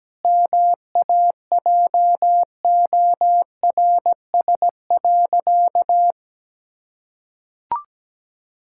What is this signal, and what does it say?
Morse code, heard as a keyed tone, about 17 wpm, 700 Hz. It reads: MAJORS.
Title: Pangram Challenge v2 - 17wpm